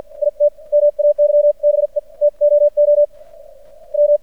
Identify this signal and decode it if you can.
Morse code, heard as a keyed tone, about 52 wpm, spaced as for 43 wpm, 590 Hz. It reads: IT MAYBE TOO M
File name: archived / CRSnet